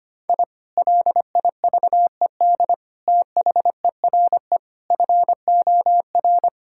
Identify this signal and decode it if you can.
Morse code, heard as a keyed tone, about 25 wpm, 700 Hz. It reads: I LIVED THERE FOR
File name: practice / 25wpm / Sentences with 3-Word Phrases - 25wpm